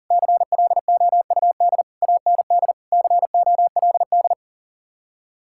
Morse code, heard as a keyed tone, about 50 wpm, 700 Hz. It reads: CLOUD AND COLD
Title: Speed-Racing Common QSO Phrases - 40wpm